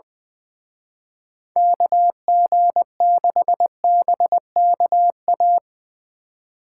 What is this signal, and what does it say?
Morse code, heard as a keyed tone, about 22 wpm, 700 Hz. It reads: KG6BKA
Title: Speed Racing - USA 2x3 Call Signs 20wpm